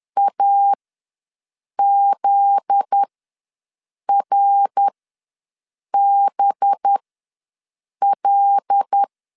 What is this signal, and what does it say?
Morse code, heard as a keyed tone, about 10 wpm, spaced as for 6 wpm, 800 Hz. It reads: AZRBL
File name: LEZIONI / 9 2 ws 10wpm 4130